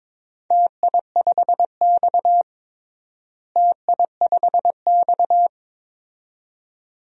Morse code, heard as a keyed tone, about 22 wpm, 700 Hz. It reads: TI5X TI5X
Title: DXCC-100 Call Signs - 4 Characters - Repeated 2x - 22wpm